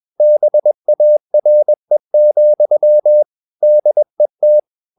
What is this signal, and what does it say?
Morse code, heard as a keyed tone, about 21 wpm, 600 Hz. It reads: BARE, DET
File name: Ferie QRS 21wpm